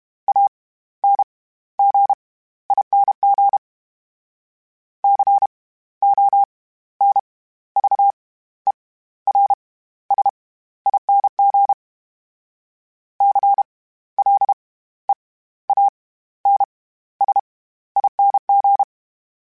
Morse code, heard as a keyed tone, about 32 wpm, spaced as for 15 wpm, 800 Hz. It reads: ANGING CONVERSING CLEANSING
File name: Special INGs / Special INGs-mp3s / SlowToFaster ing @30